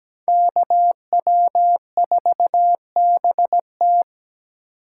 Morse code, heard as a keyed tone, about 20 wpm, 700 Hz. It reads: KW4BT
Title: Speed Racing - USA 2x2 Call Signs 17wpm